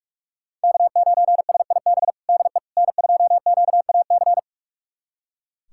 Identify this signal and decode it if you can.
Morse code, heard as a keyed tone, about 46 wpm, 700 Hz. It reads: K9SIB DE N2QAC
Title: USA Callsign DE Callsign - 5 Characters - 45wpm